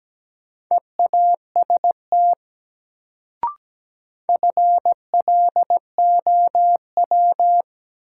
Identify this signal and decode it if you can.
Morse code, heard as a keyed tone, about 17 wpm, 700 Hz. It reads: EAST FLOW
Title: Sentences from Top 1200 Words - New Word Drill Alt - 17wpm